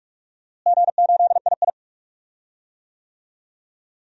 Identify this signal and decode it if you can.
Morse code, heard as a keyed tone, about 44 wpm, 700 Hz. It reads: G8II
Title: DXCC-100 Call Signs - 4 Characters - Repeated 3x - 45wpm